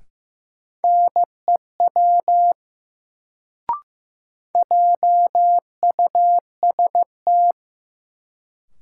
Morse code, heard as a keyed tone, about 15 wpm, 700 Hz. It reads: NEW JUST
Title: Sets of 2 Words - New Word Drill - 15wpm